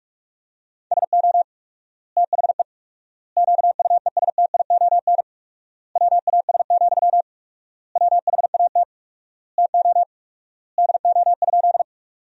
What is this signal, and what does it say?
Morse code, heard as a keyed tone, about 46 wpm, 700 Hz. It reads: SO THE QUESTION WAS, WHAT TO DO?
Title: 4x Word-Spacing - Sentences from Top 400 Words - 45wpm